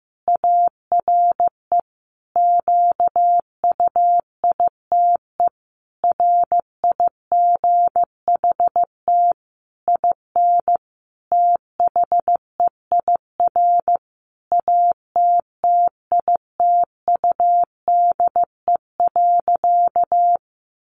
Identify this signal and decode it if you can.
Morse code, heard as a keyed tone, about 14 wpm, 700 Hz. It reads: ARE QUITE RIGHT IN THEIR ATTITUDE.
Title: Sentences from Top 1300 Words 15wpm